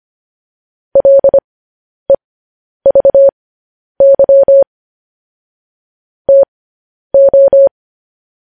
Morse code, heard as a keyed tone, about 25 wpm, spaced as for 10 wpm, 550 Hz. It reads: LEVY TO
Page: news headlines in morse code at 10 wpm